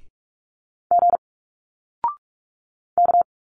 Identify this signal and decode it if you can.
Morse code, heard as a keyed tone, about 46 wpm, 700 Hz. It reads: G K
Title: Single Letters 45wpm